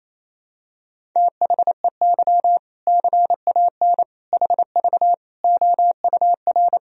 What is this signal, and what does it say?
Morse code, heard as a keyed tone, about 28 wpm, 700 Hz. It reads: THEY CAN HV OUR
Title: Sentences from Top 400 Words and Abbreviations 28wpm